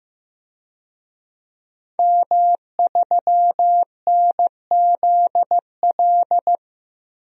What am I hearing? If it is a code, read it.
Morse code, heard as a keyed tone, about 15 wpm, 700 Hz. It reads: M3NZL
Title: DXCC-100 Call Signs - 5 Characters - 15wpm